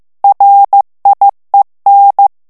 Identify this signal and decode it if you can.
Morse code, heard as a keyed tone, about 15 wpm, 800 Hz. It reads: RIEN
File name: morse mp3 / FRANCAIS TEXTE CLAIR 2 VITESSE 14 MOTS MINUTES